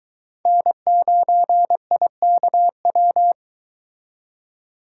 Morse code, heard as a keyed tone, about 22 wpm, 700 Hz. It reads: N9IKW